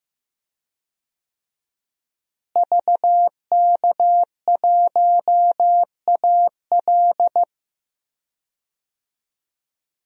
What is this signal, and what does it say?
Morse code, heard as a keyed tone, about 15 wpm, 700 Hz. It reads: VK1AL
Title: DXCC-100 Call Signs - 5 Characters - Repeated 3x - 15wpm